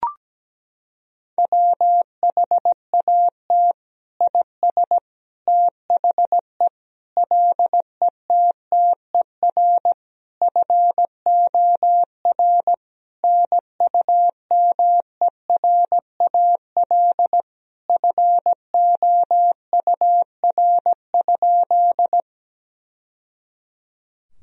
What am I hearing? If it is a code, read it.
Morse code, heard as a keyed tone, about 17 wpm, 700 Hz. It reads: WHAT IS THE LETTER FOR NUMERAL FOUR?
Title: Sentences from Top 1000 Words - Review Part 2 - 17wpm